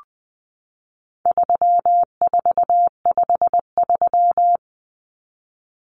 Morse code, heard as a keyed tone, about 20 wpm, 700 Hz. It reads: 3453
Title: Numbers - 4-Digit 20wpm